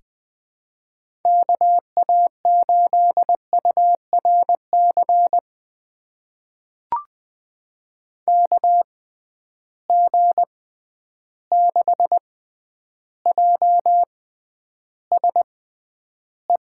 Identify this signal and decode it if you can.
Morse code, heard as a keyed tone, about 20 wpm, spaced as for 10 wpm, 700 Hz. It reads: KA8URC KG6JSE
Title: USA Call Signs - 6 Characters - 6x Character Spacing - 20wpm